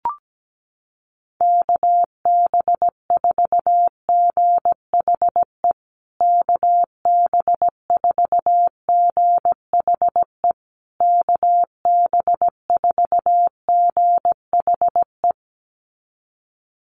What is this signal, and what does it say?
Morse code, heard as a keyed tone, about 17 wpm, 700 Hz. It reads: KB4GHE KB4GHE KB4GHE
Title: USA Call Signs Repeated 3x 17wpm